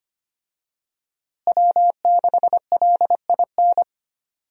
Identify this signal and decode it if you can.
Morse code, heard as a keyed tone, about 25 wpm, 700 Hz. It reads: W6LIN
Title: USA Call Signs - 5 Characters - Repeated 2x - 25wpm